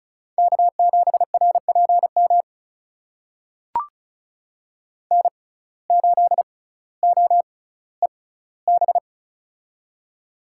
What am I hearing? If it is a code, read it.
Morse code, heard as a keyed tone, about 36 wpm, spaced as for 18 wpm, 700 Hz. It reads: K7RPM N8OEB
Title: USA Call Signs - 5 Characters - 6x Character Spacing - 35wpm